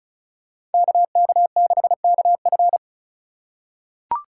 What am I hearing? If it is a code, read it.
Morse code, heard as a keyed tone, about 36 wpm, 700 Hz. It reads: KK6KF
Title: USA Call Signs - 5 Characters - 2x Character Spacing - 35wpm